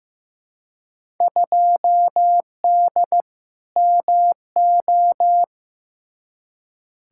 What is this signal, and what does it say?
Morse code, heard as a keyed tone, about 15 wpm, 700 Hz. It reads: 2D MO
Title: Field Day Exchanges - 15wpm